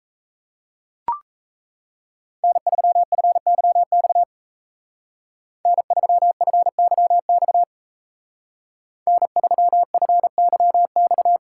Morse code, heard as a keyed tone, about 34 wpm, 700 Hz. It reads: N3FYX N3FYX N3FYX
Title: Speed Racing - USA 1x3 Call Signs 28wpm